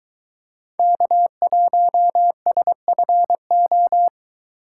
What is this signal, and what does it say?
Morse code, heard as a keyed tone, about 21 wpm, 700 Hz. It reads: K1SFO
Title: Speed Racing - USA Call Signs 17wpm